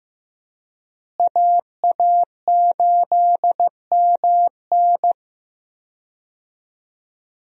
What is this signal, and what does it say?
Morse code, heard as a keyed tone, about 15 wpm, 700 Hz. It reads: AA8MN
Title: USA Call Signs - 5 Characters - Repeated 3x - 15wpm